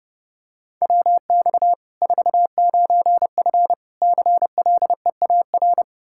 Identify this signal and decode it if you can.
Morse code, heard as a keyed tone, about 30 wpm, 700 Hz. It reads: WX 49F CLEAR
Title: Practice QSO - Exchange 2 - Info - 30wpm